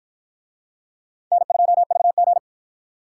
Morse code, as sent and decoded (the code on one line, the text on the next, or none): -. ..--- ...- --.
N2VG